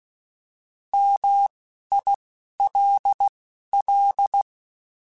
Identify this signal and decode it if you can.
Morse code, heard as a keyed tone, about 16 wpm, 800 Hz. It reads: M I L L